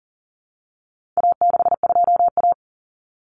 Morse code, heard as a keyed tone, about 40 wpm, 700 Hz. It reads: A62A
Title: Common Cut Numbers - 4-Digit 40wpm